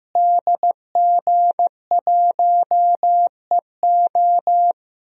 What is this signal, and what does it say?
Morse code, heard as a keyed tone, about 15 wpm, 700 Hz. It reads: DG1EO
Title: DXCC-100 Call Signs - 5 Characters - Repeated 3x - 15wpm